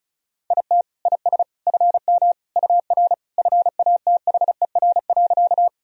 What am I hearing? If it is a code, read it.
Morse code, heard as a keyed tone, about 34 wpm, 700 Hz. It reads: IT IS FM UR FATHER.
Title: Sentences from Top 300 Words and Abbreviations 35wpm